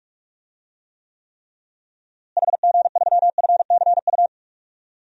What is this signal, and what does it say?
Morse code, heard as a keyed tone, about 46 wpm, 700 Hz. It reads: HG3FCU
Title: DXCC-100 Call Signs - 6 Characters - Repeated 3x - 45wpm